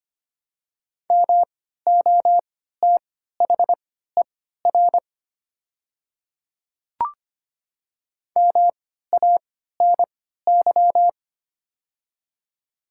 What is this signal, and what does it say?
Morse code, heard as a keyed tone, about 25 wpm, spaced as for 14 wpm, 700 Hz. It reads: MOTHER MANY
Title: 3x Character Spacing - Top 500 Words - 25wpm